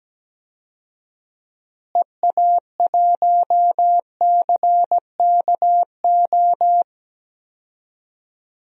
Morse code, heard as a keyed tone, about 17 wpm, 700 Hz. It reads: EA1CKO
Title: DXCC-100 Call Signs - 6 Characters - Repeated 2x - 17wpm